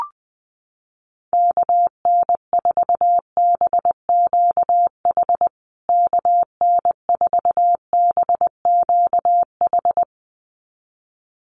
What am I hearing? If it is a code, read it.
Morse code, heard as a keyed tone, about 20 wpm, 700 Hz. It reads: KN4BQH KN4BQH
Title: USA Call Signs Repeated 2x 20wpm